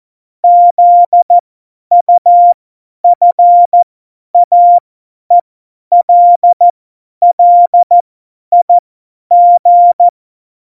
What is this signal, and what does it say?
Morse code, heard as a keyed tone, about 14 wpm, spaced as for 10 wpm, 700 Hz. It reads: ZUFAELLIG
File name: Morsekurs / Witze / Witz 06. November 2018, 50 F70